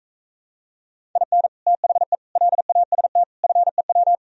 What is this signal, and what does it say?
Morse code, heard as a keyed tone, about 45 wpm, 700 Hz. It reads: IN THE LAST FEW